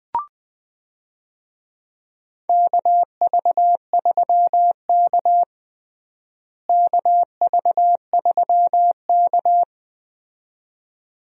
Morse code, heard as a keyed tone, about 20 wpm, 700 Hz. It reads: KV3K KV3K
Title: USA Call Signs - 4 Characters - Repeated 2x - 20wpm